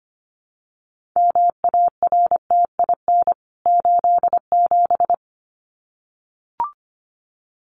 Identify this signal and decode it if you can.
Morse code, heard as a keyed tone, about 25 wpm, 690 Hz. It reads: MARTIN 87
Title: CWT Exchange 25wpm